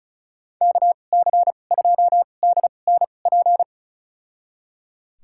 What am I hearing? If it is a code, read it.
Morse code, heard as a keyed tone, about 34 wpm, spaced as for 25 wpm, 700 Hz. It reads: KC2DNP
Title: USA Call Signs - 6 Characters - 2x Character Spacing - 35wpm